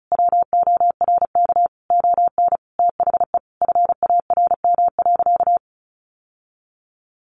Morse code, heard as a keyed tone, about 34 wpm, 700 Hz. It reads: WORK ON THE FARM.
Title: Sentences from Top 1200 Words 35wpm